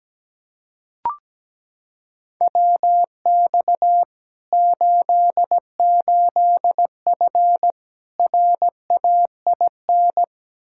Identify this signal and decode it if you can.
Morse code, heard as a keyed tone, about 17 wpm, 700 Hz. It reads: WX 88F RAIN
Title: Practice QSO - Exchange 2 - Info - 17wpm